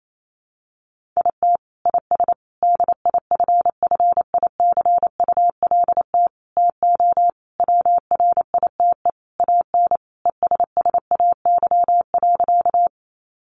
Morse code, heard as a keyed tone, about 28 wpm, 700 Hz. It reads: IT IS DIFFICULT TO WRITE AN ESSAY.
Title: Sentences from Top 1500 Words 28wpm